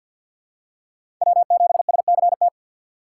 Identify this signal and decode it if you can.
Morse code, heard as a keyed tone, about 53 wpm, 700 Hz. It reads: W7SCT
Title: USA Call Signs - 5 Characters - 3x Character Spacing - 50wpm